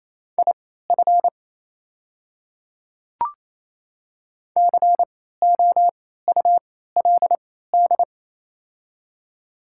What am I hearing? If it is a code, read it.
Morse code, heard as a keyed tone, about 28 wpm, spaced as for 16 wpm, 700 Hz. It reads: IF COULD
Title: 3x Character Spacing - Top 300 Words - 28wpm